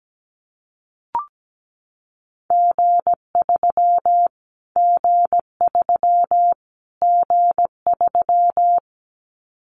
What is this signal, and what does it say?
Morse code, heard as a keyed tone, about 17 wpm, 700 Hz. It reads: G3 G3 G3
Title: Callsign Prefixes for Top 25 DXCC - Repeated 3x - 17wpm